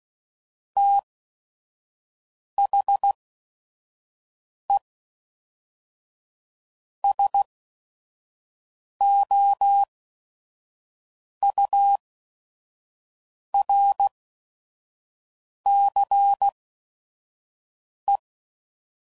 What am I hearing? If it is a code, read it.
Morse code, heard as a keyed tone, about 16 wpm, spaced as for 5 wpm, 800 Hz. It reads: THE SOURCE
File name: CW Code Groups @5WPM / Technical Subjects / XC